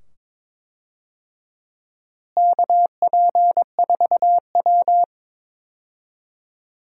Morse code, heard as a keyed tone, about 22 wpm, 700 Hz. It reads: KP4W
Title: DXCC-100 Call Signs - 4 Characters - Repeated 2x - 22wpm